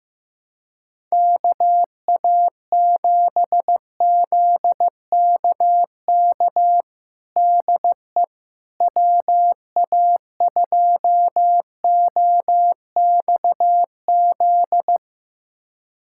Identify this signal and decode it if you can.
Morse code, heard as a keyed tone, about 15 wpm, 700 Hz. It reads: KA7ZKK DE WA2OXZ
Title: USA Callsign DE Callsign - 6 Characters - 15wpm